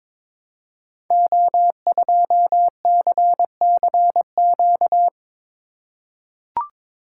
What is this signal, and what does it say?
Morse code, heard as a keyed tone, about 22 wpm, 700 Hz. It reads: O2CCQ
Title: Code Groups - 5 Letter:Numbers - 22wpm